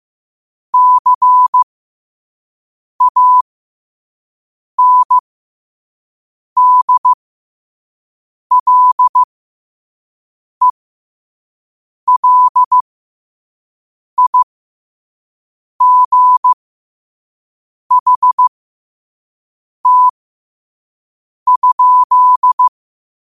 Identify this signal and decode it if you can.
Morse code, heard as a keyed tone, about 15 wpm, spaced as for 5 wpm, 1000 Hz. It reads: CANDLELIGHT?
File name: qotdfiles / 5wpm